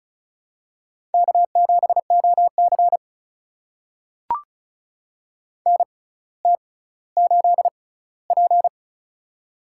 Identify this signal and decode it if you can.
Morse code, heard as a keyed tone, about 34 wpm, spaced as for 18 wpm, 700 Hz. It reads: K7OC NT8P
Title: USA Call Signs - 4 Characters - 6x Character Spacing - 35wpm